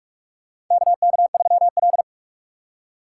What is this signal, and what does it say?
Morse code, heard as a keyed tone, about 44 wpm, 700 Hz. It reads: KK3L
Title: USA Call Signs - 4 Characters - 2x Character Spacing - 45wpm